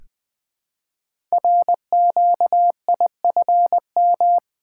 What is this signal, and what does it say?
Morse code, heard as a keyed tone, about 20 wpm, 700 Hz. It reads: RQIFM
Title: Code Groups - 5 Letters - 20wpm